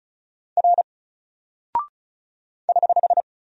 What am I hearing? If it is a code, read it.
Morse code, heard as a keyed tone, about 36 wpm, 700 Hz. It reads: R <HH>
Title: Single Character (Letter:Number:Punctuation:Prosign) 35wpm